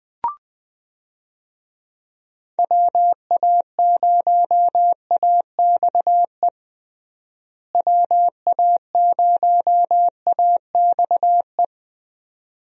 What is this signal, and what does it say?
Morse code, heard as a keyed tone, about 20 wpm, 700 Hz. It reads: WA0AXE WA0AXE
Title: USA Call Signs - 6 Characters - Repeated 2x - 20wpm